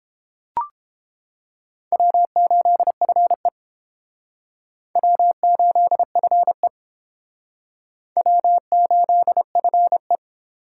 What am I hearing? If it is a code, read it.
Morse code, heard as a keyed tone, about 29 wpm, 700 Hz. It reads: W8FE W8FE W8FE